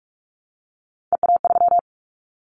Morse code, heard as a keyed tone, about 44 wpm, 700 Hz. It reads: EA3
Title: Callsign Prefixes for Top 25 DXCC - Repeated 3x - 45wpm